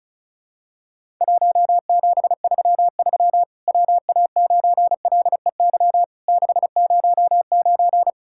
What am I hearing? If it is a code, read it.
Morse code, heard as a keyed tone, about 36 wpm, 700 Hz. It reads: 1733 WA9LEY 609
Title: Worked All Europe DX Contest - QTC Exchange - 35wpm